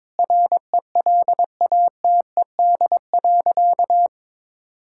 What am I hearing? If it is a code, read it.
Morse code, heard as a keyed tone, about 22 wpm, 700 Hz. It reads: RELATED.
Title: Sentences from Top 1000 Words - Review Part 5 - 22wpm